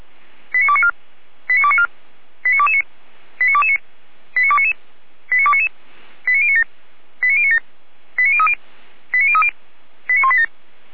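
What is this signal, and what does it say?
Five-tone selective call (ZVEI), heard as quick runs of five tones, tones of about 70 ms. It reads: 89294 89294 89229 89229 89299 89299 89097 89097 89030 89030 89187
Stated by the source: ZVEI five-tone selective call received via SDR, from the Feuerwehr Orlamünde audio server.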